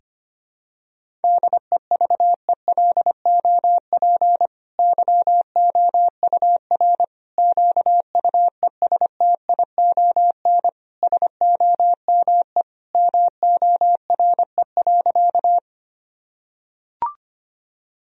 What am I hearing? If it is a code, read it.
Morse code, heard as a keyed tone, about 25 wpm, 700 Hz. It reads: DEVELOP YOUR QUESTION SOME MORE.
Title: Sentences from Top 1000 Words - Review Part 2 - 25wpm